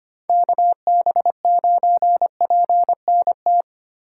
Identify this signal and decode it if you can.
Morse code, heard as a keyed tone, about 25 wpm, 700 Hz. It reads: KB9PNT
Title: USA Call Signs - 6 Characters - 4x Character Spacing - 25wpm